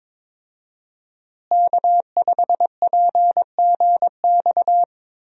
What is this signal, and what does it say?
Morse code, heard as a keyed tone, about 22 wpm, 700 Hz. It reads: K5PGX
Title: USA Call Signs - 5 Characters - Repeated 2x - 22wpm